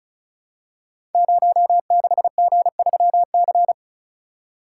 Morse code, heard as a keyed tone, about 36 wpm, 700 Hz. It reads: 06G3C
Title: Code Groups - 5 Letter:Numbers - 35wpm